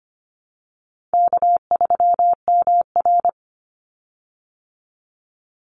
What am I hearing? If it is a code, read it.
Morse code, heard as a keyed tone, about 25 wpm, 700 Hz. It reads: K3MR
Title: Easy Call Signs Repeated 3x - 25wpm